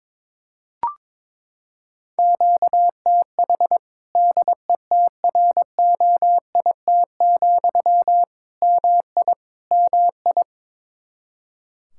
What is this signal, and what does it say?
Morse code, heard as a keyed tone, about 22 wpm, 700 Hz. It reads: QTH DETROIT, MI MI
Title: Practice QSO - Exchange 1 - Beginning - 22wpm